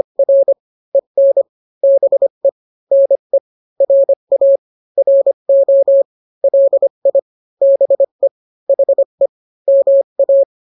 Morse code, heard as a keyed tone, about 25 wpm, 550 Hz. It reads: R EN BE NE RA RO LI BE HE MA